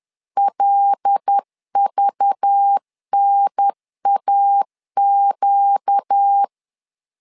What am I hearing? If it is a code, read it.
Morse code, heard as a keyed tone, about 10 wpm, 800 Hz. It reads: LVNAQ
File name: LEZIONI / 5 LBVC / 5 LBV 10wpm 2248